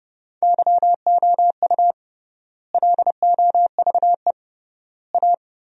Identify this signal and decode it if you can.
Morse code, heard as a keyed tone, about 31 wpm, 700 Hz. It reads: YOU LOVE A
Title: 3x Word-Spacing - Sentences from Top 500 Words - 30wpm